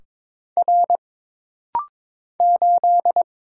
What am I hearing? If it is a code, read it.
Morse code, heard as a keyed tone, about 22 wpm, 700 Hz. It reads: R 8
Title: Single Letter-Number - Rapid-Fire 22wpm